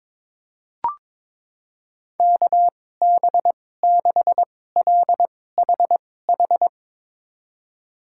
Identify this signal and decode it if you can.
Morse code, heard as a keyed tone, about 22 wpm, spaced as for 16 wpm, 700 Hz. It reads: KB6LHH
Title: USA Call Signs - 6 Characters - 2x Character Spacing - 22wpm